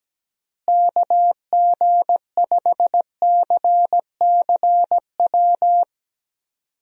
Morse code, heard as a keyed tone, about 17 wpm, 700 Hz. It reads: KG5CCW
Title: USA Call Signs - 6 Characters - Repeated 3x - 17wpm